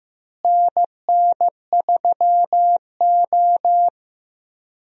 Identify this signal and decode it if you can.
Morse code, heard as a keyed tone, about 15 wpm, 700 Hz. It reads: NN3O
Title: USA Call Signs - 4 Characters - Repeated 3x - 15wpm